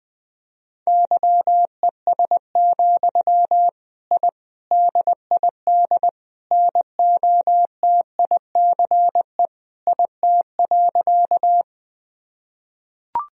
Morse code, heard as a keyed tone, about 20 wpm, 700 Hz. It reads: YES, I DID NOTICE IT.
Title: Sentences from Top 1000 Words - Review Part 2 - 20wpm